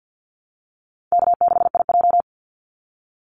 Morse code, heard as a keyed tone, about 49 wpm, 700 Hz. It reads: K6IJ
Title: Easy Call Signs Repeated 3x - 50wpm